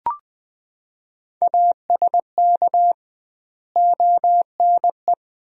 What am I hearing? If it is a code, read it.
Morse code, heard as a keyed tone, about 20 wpm, 700 Hz. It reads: ASK ONE
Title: 2x Word-Spacing - Sentences from Top 300 Words - 20wpm